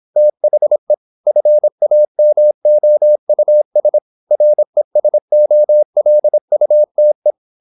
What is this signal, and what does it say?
Morse code, heard as a keyed tone, about 26 wpm, 600 Hz. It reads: THE FAMOUS RESOLUTE
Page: Never 26wpm | CW med Gnister